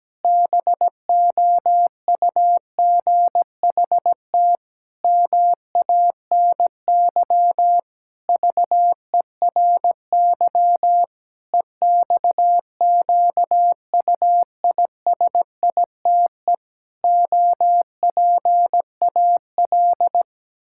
Morse code, heard as a keyed tone, about 17 wpm, 700 Hz. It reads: BOUGHT MANY VERY EXQUISITE OPAL
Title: Pangram Challenge v2 - 17wpm